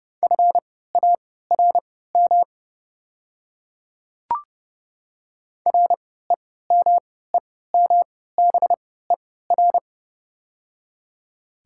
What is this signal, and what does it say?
Morse code, heard as a keyed tone, about 30 wpm, spaced as for 17 wpm, 700 Hz. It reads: FARM REMEMBER
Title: 3x Character Spacing - Top 500 Words - 30wpm